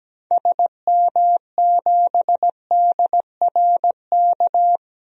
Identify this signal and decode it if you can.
Morse code, heard as a keyed tone, about 17 wpm, 700 Hz. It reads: SM7DRK
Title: DXCC-100 Call Signs - 6 Characters - Repeated 2x - 17wpm